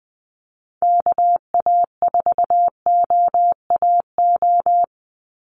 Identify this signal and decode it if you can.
Morse code, heard as a keyed tone, about 20 wpm, 700 Hz. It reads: KA4OAO
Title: USA Call Signs 20wpm